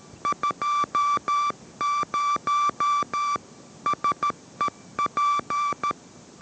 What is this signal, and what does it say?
Morse code, heard as a keyed tone, about 13 wpm, 1170 Hz. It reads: 20 SEP